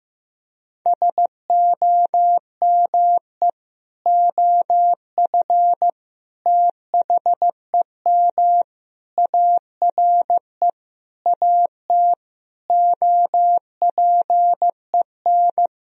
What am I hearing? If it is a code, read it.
Morse code, heard as a keyed tone, about 16 wpm, 700 Hz. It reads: SOME OF THEM ARE AT OPEN
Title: Sentences from Top 1000 Words - Review Part 2 - 15wpm